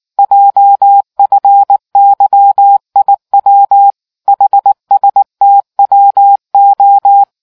Morse code, heard as a keyed tone, about 19 wpm, 800 Hz. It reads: JFYIW HSTWO